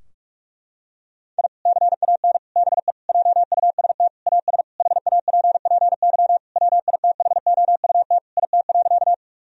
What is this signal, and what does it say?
Morse code, heard as a keyed tone, about 43 wpm, 700 Hz. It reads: I CAN BE JUST AS HAPPY WITHOUT IT.
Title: Sentences from Top 1000 Words - Review Part 3 - 45wpm